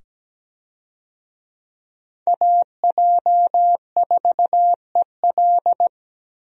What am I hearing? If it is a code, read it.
Morse code, heard as a keyed tone, about 17 wpm, 700 Hz. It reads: AJ4EL
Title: USA Call Signs - 5 Characters - 17wpm